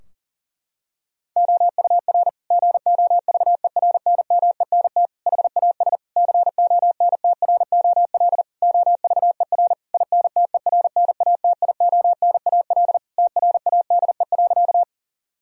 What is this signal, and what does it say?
Morse code, heard as a keyed tone, about 40 wpm, 700 Hz. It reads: OUR GOVERNMENT HAS CONTROL OVER INTERNATIONAL TRADE.
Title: Sentences from Top 1000 Words - Review Part 4 - 40wpm